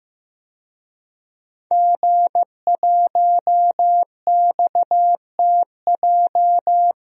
Morse code, heard as a keyed tone, about 15 wpm, 700 Hz. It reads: G1XTJ
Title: DXCC-100 Call Signs - 5 Characters - Repeated 3x - 15wpm